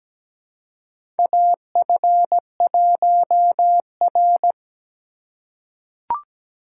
Morse code, heard as a keyed tone, about 17 wpm, 700 Hz. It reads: AF1R
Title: USA Call Signs - 4 Characters - 8x Character Spacing - 17wpm